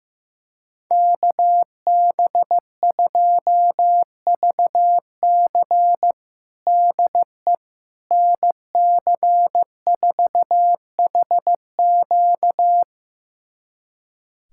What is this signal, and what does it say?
Morse code, heard as a keyed tone, about 15 wpm, 700 Hz. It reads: KB2VC DE NC4HQ
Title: USA Callsign DE Callsign - 5 Characters - 15wpm